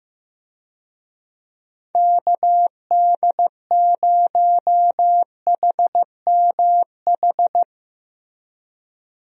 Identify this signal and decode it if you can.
Morse code, heard as a keyed tone, about 15 wpm, 700 Hz. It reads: KD0HMH
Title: USA Call Signs - 6 Characters - Repeated 2x - 15wpm